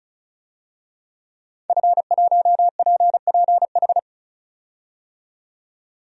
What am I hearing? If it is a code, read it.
Morse code, heard as a keyed tone, about 36 wpm, 700 Hz. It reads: F1PPH
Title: DXCC-100 Call Signs - 5 Characters - 35wpm